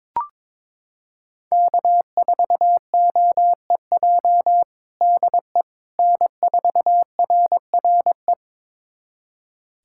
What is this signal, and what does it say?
Morse code, heard as a keyed tone, about 22 wpm, 700 Hz. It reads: K4OEJ DE N4RRE
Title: USA Callsign DE Callsign - 5 Characters - 22wpm